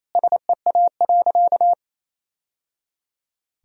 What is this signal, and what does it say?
Morse code, heard as a keyed tone, about 28 wpm, 700 Hz. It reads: SEA.
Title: Sentences from Top 1000 Words - Review Part 1 - 28wpm